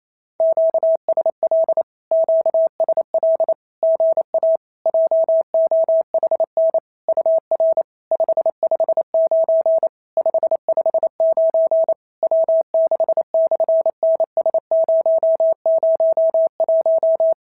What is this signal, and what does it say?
Morse code, heard as a keyed tone, about 28 wpm, 650 Hz. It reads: QSL QSL GA JOHN UR 559 559 W6/NS001
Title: SOTA QSO Practice 28wpm